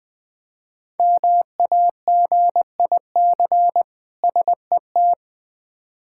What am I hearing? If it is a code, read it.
Morse code, heard as a keyed tone, about 20 wpm, 700 Hz. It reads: MAGIC SET
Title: Difficult Words with Hints - 20wpm